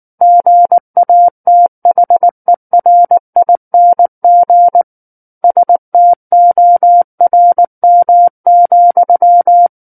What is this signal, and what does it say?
Morse code, heard as a keyed tone, about 19 wpm, 700 Hz. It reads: GATHERING STORM,
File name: Book One / 10-Chapter 10 - 19 WPM